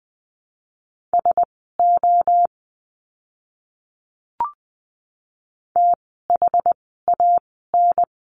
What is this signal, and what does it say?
Morse code, heard as a keyed tone, about 20 wpm, spaced as for 15 wpm, 700 Hz. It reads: SO THAN
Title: 185 Lesson 30f - Letter b - Words 20wpm